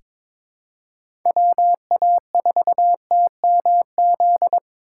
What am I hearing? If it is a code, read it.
Morse code, heard as a keyed tone, about 23 wpm, 700 Hz. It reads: WA4TMZ
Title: USA Call Signs - 6 Characters - 2x Character Spacing - 22wpm